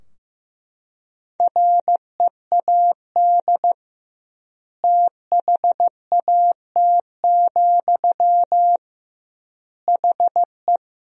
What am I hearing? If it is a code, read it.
Morse code, heard as a keyed tone, about 15 wpm, 700 Hz. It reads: READ THAT, HE
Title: 2x Word-Spacing - Sentences from Top 200 Words - 15wpm